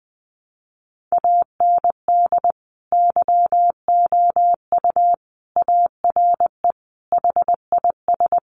Morse code, heard as a keyed tone, about 20 wpm, 700 Hz. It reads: AND YOU ARE HIS
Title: Sentences from Top 1300 Words 20wpm